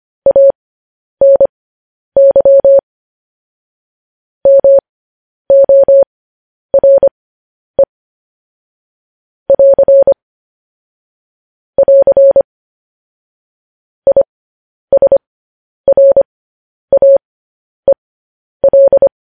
Morse code, heard as a keyed tone, about 25 wpm, spaced as for 10 wpm, 550 Hz. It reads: ANY MORE <AR> <AR> ISRAEL
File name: podcast / News headlines-1x-25x10